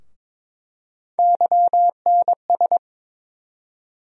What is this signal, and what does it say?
Morse code, heard as a keyed tone, about 22 wpm, 700 Hz. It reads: YNS
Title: USA Call Sign Postfixes - 3 Letters 22wpm